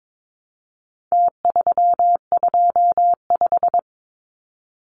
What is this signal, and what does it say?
Morse code, heard as a keyed tone, about 22 wpm, 700 Hz. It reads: T325
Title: Numbers - Contest Style - 4-Digit 22wpm